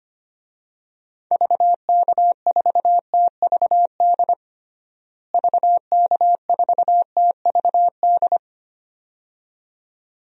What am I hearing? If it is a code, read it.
Morse code, heard as a keyed tone, about 25 wpm, 700 Hz. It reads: VK4TVD VK4TVD
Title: DXCC-100 Call Signs - 6 Characters - Repeated 2x - 25wpm